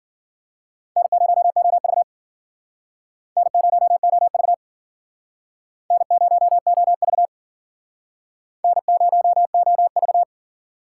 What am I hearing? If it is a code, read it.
Morse code, heard as a keyed tone, about 50 wpm, 700 Hz. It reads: N0OV N0OV N0OV N0OV
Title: Speed Racing - USA 1x2 Call Signs 40wpm